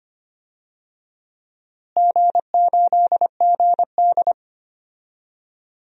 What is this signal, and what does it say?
Morse code, heard as a keyed tone, about 25 wpm, 700 Hz. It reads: G8GD
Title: DXCC-100 Call Signs - 4 Characters - Repeated 3x - 25wpm